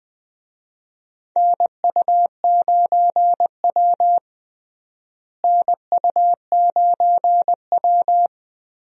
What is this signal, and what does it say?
Morse code, heard as a keyed tone, about 20 wpm, 690 Hz. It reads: NU9W NU9W
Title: USA Call Signs - 4 Characters - Repeated 3x - 20wpm